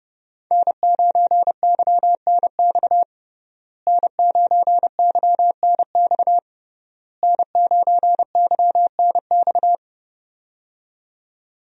Morse code, heard as a keyed tone, about 31 wpm, 700 Hz. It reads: N9YNX N9YNX N9YNX
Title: USA Call Signs - 5 Characters - Repeated 3x - 30wpm